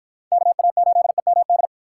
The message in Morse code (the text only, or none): KA8EMD